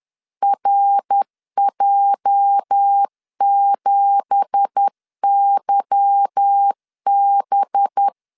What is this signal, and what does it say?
Morse code, heard as a keyed tone, about 10 wpm, 800 Hz. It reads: RJ7YB
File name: LEZIONI / 9 2867 10wpm 2733